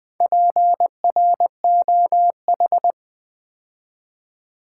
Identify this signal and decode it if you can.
Morse code, heard as a keyed tone, about 20 wpm, 700 Hz. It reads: PROH
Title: Build A Word - 2 Letters At A Time - 8-Letter Words - 20wpm